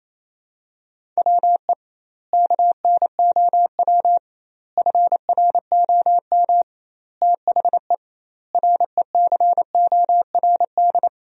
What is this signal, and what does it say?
Morse code, heard as a keyed tone, about 28 wpm, 700 Hz. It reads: WE KNOW FROM THE RECORD
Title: 2x Word-Spacing - Sentences from Top 500 Words - 28wpm